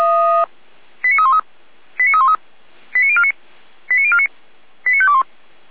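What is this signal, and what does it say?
Five-tone selective call (ZVEI), heard as quick runs of five tones, tones of about 70 ms. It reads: 89313 89313 89049 89049 89521
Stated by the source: ZVEI five-tone selective call, code 89305, received on an SDR by the Feuerwehr Orlamünde fire brigade.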